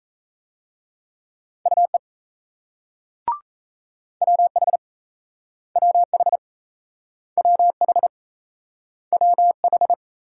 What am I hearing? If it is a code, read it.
Morse code, heard as a keyed tone, about 38 wpm, 700 Hz. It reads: UE WH WH WH WH